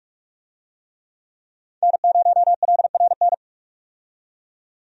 Morse code, heard as a keyed tone, about 43 wpm, 700 Hz. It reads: N0LRN
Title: USA Call Signs - 5 Characters - 45wpm